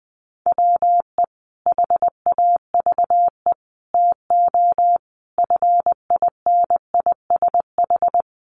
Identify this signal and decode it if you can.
Morse code, heard as a keyed tone, about 20 wpm, 700 Hz. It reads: WE HAVE TO FINISH
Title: Sentences from Top 1200 Words 20wpm